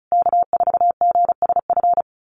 Morse code, heard as a keyed tone, about 34 wpm, 700 Hz. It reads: K4GSF
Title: USA Call Signs Repeated 2x 35wpm